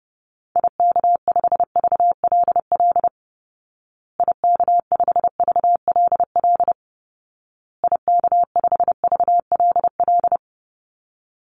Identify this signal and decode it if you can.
Morse code, heard as a keyed tone, about 30 wpm, 700 Hz. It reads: IK5VLL IK5VLL IK5VLL
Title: Call Signs Repeated 2x - 30wpm